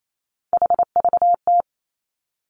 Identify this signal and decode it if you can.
Morse code, heard as a keyed tone, about 28 wpm, 700 Hz. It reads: HVT
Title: Three Challenging Letter-Number Combinations 28wpm